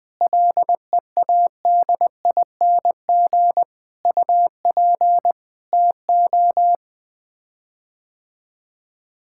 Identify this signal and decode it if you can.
Morse code, heard as a keyed tone, about 21 wpm, 700 Hz. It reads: LEADING UP TO
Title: Sets of 6 Words - 20wpm